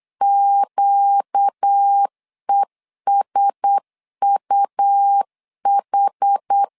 Morse code, heard as a keyed tone, about 8 wpm, 800 Hz. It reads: QESUH
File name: LEZIONI / 4 Q1S 8wpm 2710